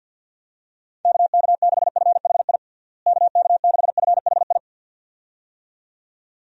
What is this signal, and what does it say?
Morse code, heard as a keyed tone, about 46 wpm, 700 Hz. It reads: KK6FHI KK6FHI
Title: USA Call Signs - 6 Characters - Repeated 2x - 50wpm